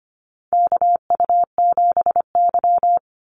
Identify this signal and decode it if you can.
Morse code, heard as a keyed tone, about 25 wpm, 700 Hz. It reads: KU7Y
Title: Call Signs Repeated 2x - 25wpm